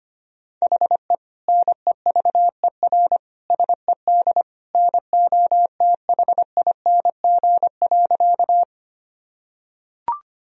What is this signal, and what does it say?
Morse code, heard as a keyed tone, about 25 wpm, 700 Hz. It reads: HE NEVER SED NOTHING.
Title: Sentences from Top 500 Words and Abbreviations 25wpm